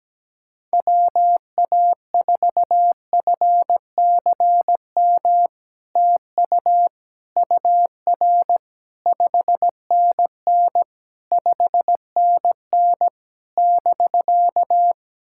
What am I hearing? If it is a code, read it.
Morse code, heard as a keyed tone, about 17 wpm, 700 Hz. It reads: WA4FCM TU UR 5NN 5NN <BK>
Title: SOTA QSO Practice 17wpm